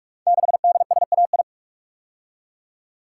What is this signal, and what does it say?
Morse code, heard as a keyed tone, about 44 wpm, 700 Hz. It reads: 6DSAI
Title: Code Groups - 5 Letter:Numbers - 45wpm